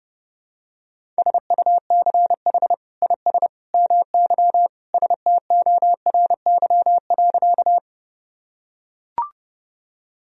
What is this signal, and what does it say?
Morse code, heard as a keyed tone, about 32 wpm, 700 Hz. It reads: SUCH IS MY STORY.
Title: Sentences from Top 1000 Words - Review Part 1 - 30wpm